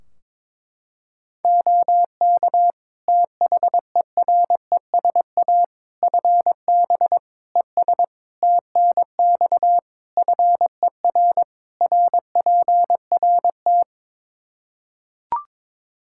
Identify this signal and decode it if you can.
Morse code, heard as a keyed tone, about 22 wpm, 700 Hz. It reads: OK THERESA FB ES TNX FER RPRT
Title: Practice QSO - Exchange 2 - Info - 22wpm